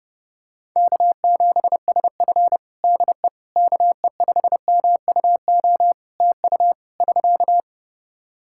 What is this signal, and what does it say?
Morse code, heard as a keyed tone, about 29 wpm, 700 Hz. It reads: K7SF DE KE5MUO TU <SK>
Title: K1USN SST QSO Practice 30wpm